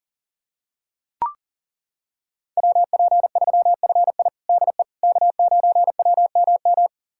Morse code, heard as a keyed tone, about 40 wpm, 700 Hz. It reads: WP3FI DE K9WMM
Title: USA Callsign DE Callsign - 5 Characters - 40wpm